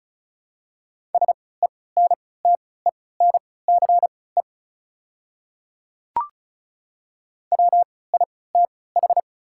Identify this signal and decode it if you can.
Morse code, heard as a keyed tone, about 34 wpm, spaced as for 20 wpm, 700 Hz. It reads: SENTENCE WITH
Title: 3x Character Spacing - Top 500 Words - 35wpm